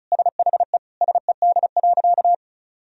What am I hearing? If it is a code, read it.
Morse code, heard as a keyed tone, about 36 wpm, 700 Hz. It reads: SHE SED.
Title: Sentences from Top 100 Words and Abbreviations 35wpm